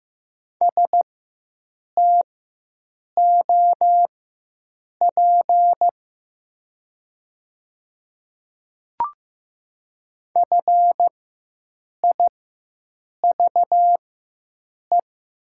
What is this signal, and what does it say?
Morse code, heard as a keyed tone, about 15 wpm, spaced as for 7 wpm, 700 Hz. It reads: STOP FIVE
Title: 4x Character Spacing - Top 500 Words - 15wpm